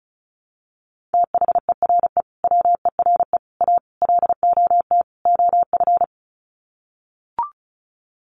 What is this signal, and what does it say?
Morse code, heard as a keyed tone, about 34 wpm, 700 Hz. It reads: THERE WERE A LOT OF
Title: Sets of 5 Words 35wpm